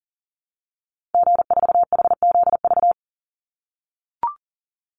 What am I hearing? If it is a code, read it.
Morse code, heard as a keyed tone, about 37 wpm, 700 Hz. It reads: G4HZV
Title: Call Signs - 40wpm